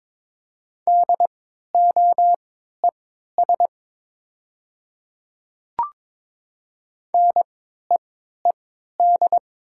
Morse code, heard as a keyed tone, about 22 wpm, spaced as for 12 wpm, 700 Hz. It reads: DOES NEED
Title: 3x Character Spacing - Top 300 Words - 22wpm